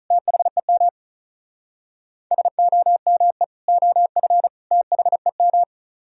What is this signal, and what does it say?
Morse code, heard as a keyed tone, about 42 wpm, 700 Hz. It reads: THEM SOME OF THEM
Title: Speed-Racing - Sets of 3 Words 35wpm